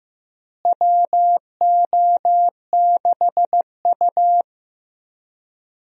Morse code, heard as a keyed tone, about 15 wpm, 700 Hz. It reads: WO6U
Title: USA Call Signs - 4 Characters - Repeated 3x - 15wpm